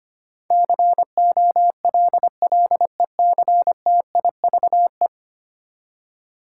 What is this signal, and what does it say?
Morse code, heard as a keyed tone, about 25 wpm, 700 Hz. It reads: COLLECTIVE
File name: Ten Letter Words - 25wpm